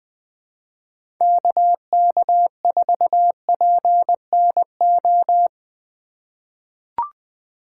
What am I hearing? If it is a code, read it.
Morse code, heard as a keyed tone, about 20 wpm, 700 Hz. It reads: KK4PNO
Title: USA Call Signs - 6 Characters - 2x Character Spacing - 20wpm